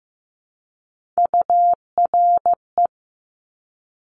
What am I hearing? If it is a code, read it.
Morse code, heard as a keyed tone, about 15 wpm, 700 Hz. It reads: URE
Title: Top Three Letter Combinations in Words 15wpm